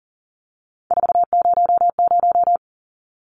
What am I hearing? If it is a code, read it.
Morse code, heard as a keyed tone, about 40 wpm, 700 Hz. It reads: 400
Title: Numbers - 3-Digit 40wpm